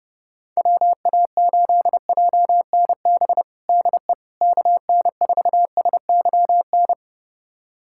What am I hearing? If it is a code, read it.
Morse code, heard as a keyed tone, about 32 wpm, 700 Hz. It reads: WA8JNB DE KN4SYN
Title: USA Callsign DE Callsign - 6 Characters - 30wpm